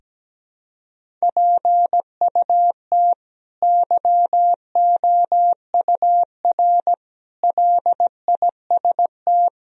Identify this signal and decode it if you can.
Morse code, heard as a keyed tone, about 17 wpm, 700 Hz. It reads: PUT YOUR LIST
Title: Sentences from Top 1000 Words - Review Part 2 - 17wpm